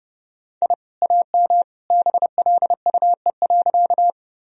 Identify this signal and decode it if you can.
Morse code, heard as a keyed tone, about 30 wpm, 700 Hz. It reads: I AM BLUE.
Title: Sentences from Top 1000 Words - Review Part 3 - 30wpm